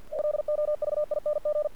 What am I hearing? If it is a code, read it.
Morse code, heard as a keyed tone, about 50 wpm, 600 Hz. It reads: LOVING
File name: archived / CRSnet